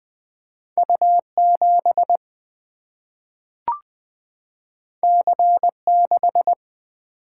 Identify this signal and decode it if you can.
Morse code, heard as a keyed tone, about 20 wpm, 700 Hz. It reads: U 7 C 6
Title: Single-Letter Single-Number 20wpm